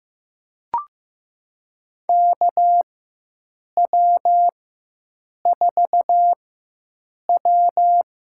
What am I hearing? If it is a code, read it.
Morse code, heard as a keyed tone, about 15 wpm, spaced as for 7 wpm, 700 Hz. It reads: KW4W
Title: USA Call Signs - 4x Character Spacing - 15wpm